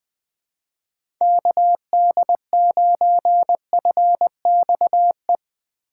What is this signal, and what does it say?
Morse code, heard as a keyed tone, about 20 wpm, 700 Hz. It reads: KD9FXE
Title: USA Call Signs - 6 Characters - 4x Character Spacing - 20wpm